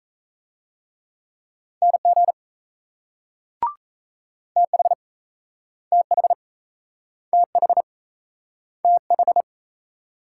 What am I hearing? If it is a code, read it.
Morse code, heard as a keyed tone, about 38 wpm, 700 Hz. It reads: NG TH TH TH TH